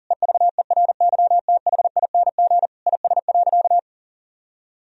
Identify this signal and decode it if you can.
Morse code, heard as a keyed tone, about 40 wpm, 700 Hz. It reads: EVERYTHING IS.
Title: Sentences from Top 1000 Words - Review Part 4 - 40wpm